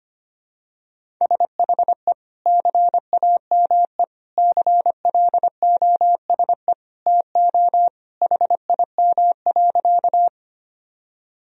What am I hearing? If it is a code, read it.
Morse code, heard as a keyed tone, about 25 wpm, 700 Hz. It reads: SHE CAME CLOSE TO HIM.
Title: Sentences from Top 1000 Words - Review Part 1 - 25wpm